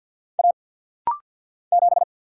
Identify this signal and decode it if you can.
Morse code, heard as a keyed tone, about 48 wpm, 700 Hz. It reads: A 7
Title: Single Letter-Number - Rapid-Fire 50wpm